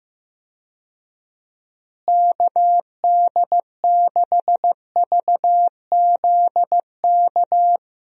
Morse code, heard as a keyed tone, about 15 wpm, 700 Hz. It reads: KD6VZK
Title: USA Call Signs - 6 Characters - Repeated 3x - 15wpm